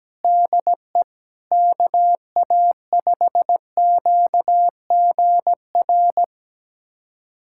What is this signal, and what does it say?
Morse code, heard as a keyed tone, about 17 wpm, 700 Hz. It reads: DE KA5QGR
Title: USA Callsign DE Callsign 17wpm